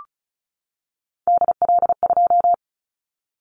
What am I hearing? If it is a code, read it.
Morse code, heard as a keyed tone, about 36 wpm, 700 Hz. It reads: DL2
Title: Callsign Prefixes for Top 25 DXCC - 35wpm